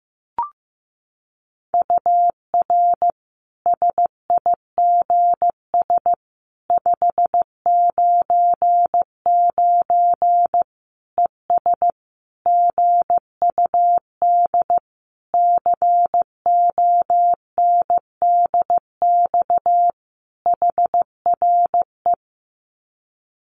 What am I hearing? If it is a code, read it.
Morse code, heard as a keyed tone, about 15 wpm, 700 Hz. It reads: UR SIGS 599 ES GUD CONDX HRE
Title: Common QSO Phrases 15wpm